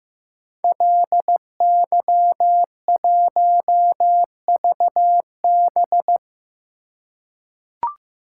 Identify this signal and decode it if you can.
Morse code, heard as a keyed tone, about 15 wpm, 700 Hz. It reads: LY1VB
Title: Code Groups - 5 Letter:Numbers - 15wpm